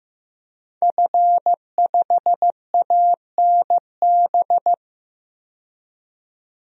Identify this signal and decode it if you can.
Morse code, heard as a keyed tone, about 15 wpm, 700 Hz. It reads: F5ANB
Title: DXCC-100 Call Signs - 5 Characters - 15wpm